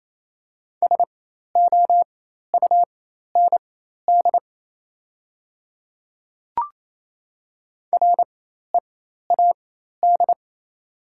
Morse code, heard as a keyed tone, about 28 wpm, spaced as for 13 wpm, 700 Hz. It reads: SOUND READ
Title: 4x Character Spacing - Top 200 Words - 28wpm